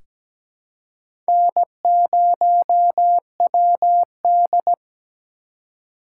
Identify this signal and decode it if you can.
Morse code, heard as a keyed tone, about 17 wpm, 700 Hz. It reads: N0WD
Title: USA Call Signs - 4 Characters - 3x Character Spacing - 17wpm